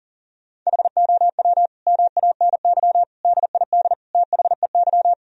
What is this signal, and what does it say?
Morse code, heard as a keyed tone, about 40 wpm, 700 Hz. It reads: HOW MANY DID THEY
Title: Sentences from Top 1000 Words - Review Part 1 - 40wpm